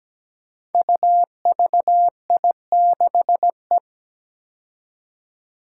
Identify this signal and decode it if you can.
Morse code, heard as a keyed tone, about 17 wpm, 700 Hz. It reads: UVI6E
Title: Code Groups - 5 Letter:Numbers - 17wpm